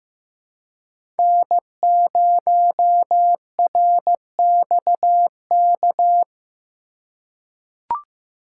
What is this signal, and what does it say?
Morse code, heard as a keyed tone, about 15 wpm, 700 Hz. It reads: N0RXK
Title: USA Call Signs - 5 Characters - 3x Character Spacing - 15wpm